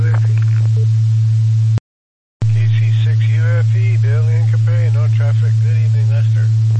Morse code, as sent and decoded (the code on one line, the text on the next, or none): none